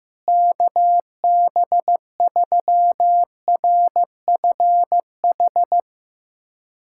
Morse code, heard as a keyed tone, about 15 wpm, 700 Hz. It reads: KB3RFH
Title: USA Callsign DE Callsign 15wpm